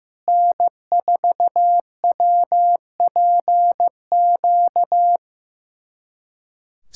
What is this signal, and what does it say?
Morse code, heard as a keyed tone, about 16 wpm, 700 Hz. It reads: N4WPQ
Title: SOTA QSO Practice 15wpm